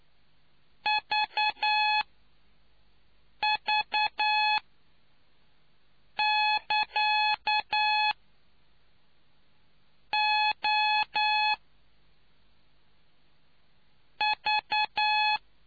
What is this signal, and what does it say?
Morse code, heard as a keyed tone, about 9 wpm, spaced as for 4 wpm, 850 Hz. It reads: VV<KA> O V